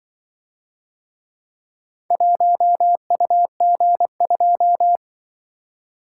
Morse code, heard as a keyed tone, about 25 wpm, 700 Hz. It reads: 1UG2
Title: Speed-Racing - Four Letter-Number Combinations - 20wpm